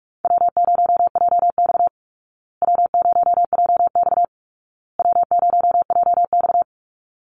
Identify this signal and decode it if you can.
Morse code, heard as a keyed tone, about 44 wpm, 700 Hz. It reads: W0JX W0JX W0JX
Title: Call Signs Repeated 2x - 45wpm